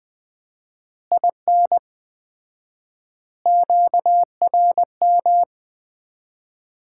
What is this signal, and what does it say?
Morse code, heard as a keyed tone, about 20 wpm, 700 Hz. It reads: IN QRM
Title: Common QSO Phrases - 4x Word-Spacing - 20wpm